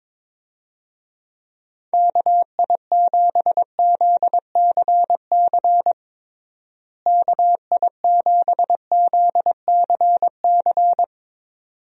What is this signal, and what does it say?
Morse code, heard as a keyed tone, about 22 wpm, 700 Hz. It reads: KI7ZCC KI7ZCC
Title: USA Call Signs - 6 Characters - Repeated 2x - 22wpm